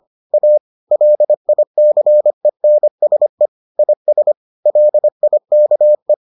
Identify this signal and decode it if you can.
Morse code, heard as a keyed tone, about 25 wpm, 600 Hz. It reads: A LICENSE IS LIKE